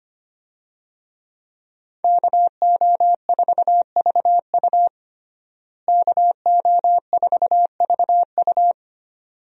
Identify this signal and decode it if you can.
Morse code, heard as a keyed tone, about 25 wpm, 700 Hz. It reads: KO4VU KO4VU
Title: USA Call Signs - 5 Characters - Repeated 2x - 25wpm